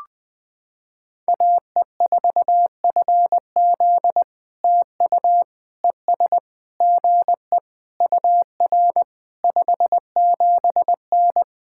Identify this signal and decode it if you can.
Morse code, heard as a keyed tone, about 21 wpm, 700 Hz. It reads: AE4FZ TU ES GE UR 57N
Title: POTA QSO Practice - 20wpm